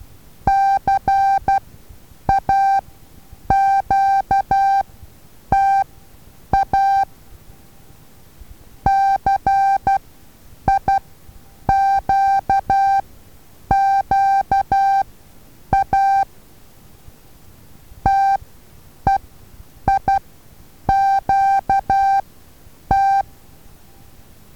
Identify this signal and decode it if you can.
Morse code, heard as a keyed tone, about 12 wpm, spaced as for 8 wpm, 780 Hz. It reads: CAQTA CIQQA TEIQT
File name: Taviro HA6VA A